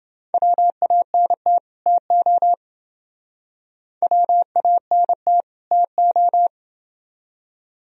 Morse code, heard as a keyed tone, about 29 wpm, 700 Hz. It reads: WANT TO WANT TO